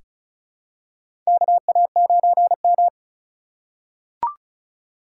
Morse code, heard as a keyed tone, about 36 wpm, 700 Hz. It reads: KA9M
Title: USA Call Signs - 4 Characters - 8x Character Spacing - 35wpm